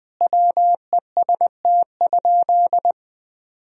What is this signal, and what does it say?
Morse code, heard as a keyed tone, about 20 wpm, 700 Hz. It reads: WEST?
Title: Sentences from Top 1000 Words - Review Part 2 - 20wpm